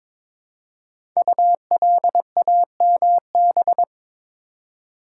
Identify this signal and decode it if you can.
Morse code, heard as a keyed tone, about 22 wpm, 700 Hz. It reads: ULAMB
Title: Code Groups - 5 Letters - 22wpm